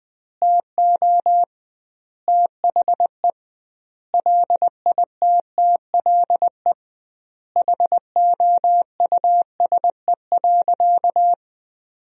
Morse code, heard as a keyed tone, about 20 wpm, 700 Hz. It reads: TO THE LITTLE HOUSE.
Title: 2x Word-Spacing - Sentences from Top 200 Words - 20wpm